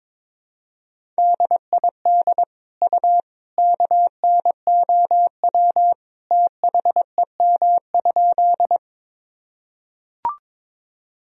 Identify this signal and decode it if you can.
Morse code, heard as a keyed tone, about 22 wpm, 700 Hz. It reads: DID U KNOW THEM?
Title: Sentences from Top 100 Words and Abbreviations 22wpm